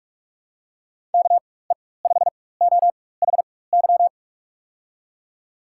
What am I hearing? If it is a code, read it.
Morse code, heard as a keyed tone, about 47 wpm, spaced as for 21 wpm, 700 Hz. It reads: KE5OHY
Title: USA Call Signs - 6 Characters - 4x Character Spacing - 45wpm